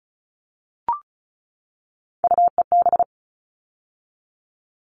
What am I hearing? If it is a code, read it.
Morse code, heard as a keyed tone, about 34 wpm, 700 Hz. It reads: UEB
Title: All Cut Numbers - 3-Digit 35wpm